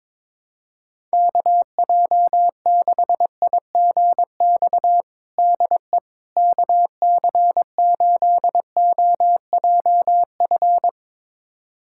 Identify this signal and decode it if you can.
Morse code, heard as a keyed tone, about 22 wpm, 700 Hz. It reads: KJ6IGX DE KC8OJF
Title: USA Callsign DE Callsign - 6 Characters - 22wpm